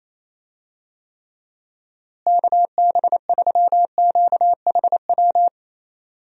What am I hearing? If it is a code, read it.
Morse code, heard as a keyed tone, about 27 wpm, 700 Hz. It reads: KB3QHW
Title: USA Call Signs - 6 Characters - Repeated 2x - 28wpm